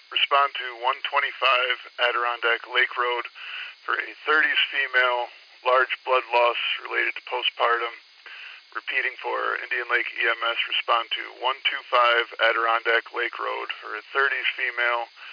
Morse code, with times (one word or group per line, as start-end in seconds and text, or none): none